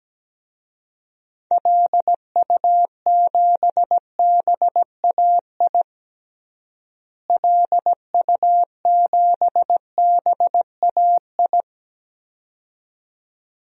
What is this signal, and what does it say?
Morse code, heard as a keyed tone, about 17 wpm, 700 Hz. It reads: LU7BAI LU7BAI
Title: DXCC-100 Call Signs - 6 Characters - Repeated 2x - 17wpm